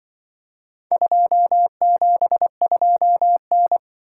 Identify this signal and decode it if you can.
Morse code, heard as a keyed tone, about 24 wpm, 700 Hz. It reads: 272N
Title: Speed-Racing - Four Letter-Number Combinations - 20wpm